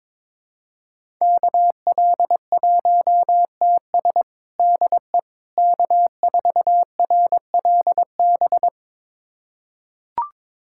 Morse code, heard as a keyed tone, about 22 wpm, 700 Hz. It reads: KL1TS DE K4RLB
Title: USA Callsign DE Callsign 22wpm